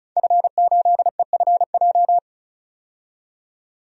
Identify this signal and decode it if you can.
Morse code, heard as a keyed tone, about 36 wpm, 700 Hz. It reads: F8EFJ
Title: DXCC-100 Call Signs - 5 Characters - 35wpm